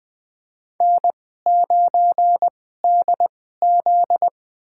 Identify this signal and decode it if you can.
Morse code, heard as a keyed tone, about 20 wpm, spaced as for 14 wpm, 700 Hz. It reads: N9DZ
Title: USA Call Signs - 4 Characters - 2x Character Spacing - 20wpm